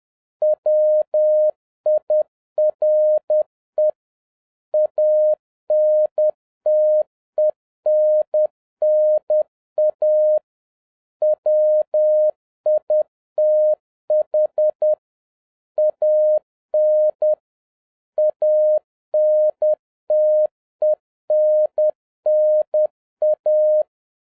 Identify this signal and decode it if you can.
Morse code, heard as a keyed tone, about 10 wpm, 610 Hz. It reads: WIRE ANTENNA WITH AN ANTENNA
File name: 2011-07-mp3 / n41 EXCITING QRP CW OPERATION 10wpm